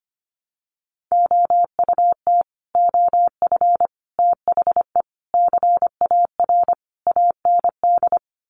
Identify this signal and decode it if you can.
Morse code, heard as a keyed tone, about 25 wpm, 700 Hz. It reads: OUT OF THE CAR AND
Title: Sets of 5 Words Encore 25wpm